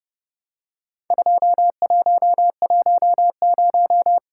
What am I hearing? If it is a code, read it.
Morse code, heard as a keyed tone, about 30 wpm, 700 Hz. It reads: 2110
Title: Worked All Europe DX Contest - QTC Exchange - 30wpm